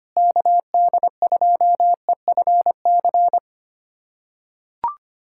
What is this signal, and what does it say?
Morse code, heard as a keyed tone, about 25 wpm, 700 Hz. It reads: KD2EFC
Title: USA Callsign DE Callsign 25wpm